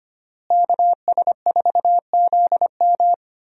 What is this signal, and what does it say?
Morse code, heard as a keyed tone, about 25 wpm, 700 Hz. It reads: KS4ZM
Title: USA Call Signs - 5 Characters - Repeated 2x - 25wpm